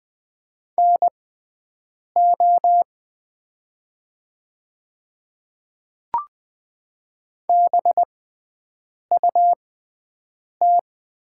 Morse code, heard as a keyed tone, about 20 wpm, spaced as for 7 wpm, 700 Hz. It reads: NO BUT
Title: 6x Character Spacing - Top 100 Words - 20wpm